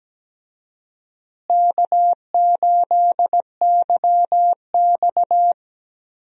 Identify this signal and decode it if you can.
Morse code, heard as a keyed tone, about 17 wpm, 700 Hz. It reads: K8YX
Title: USA Call Signs - 4 Characters - Repeated 2x - 17wpm